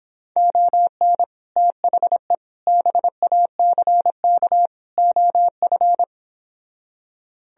Morse code, heard as a keyed tone, about 26 wpm, 700 Hz. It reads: ON THE BACK OF